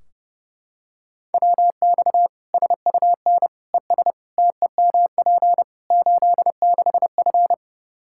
Morse code, heard as a keyed tone, about 30 wpm, 700 Hz. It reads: WX SUN ES TEMP 86F
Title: K1USN SST QSO Practice 30wpm